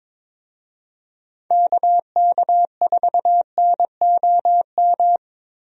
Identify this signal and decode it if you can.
Morse code, heard as a keyed tone, about 22 wpm, 700 Hz. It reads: KK4NOM
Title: USA Call Signs - 6 Characters - Repeated 3x - 22wpm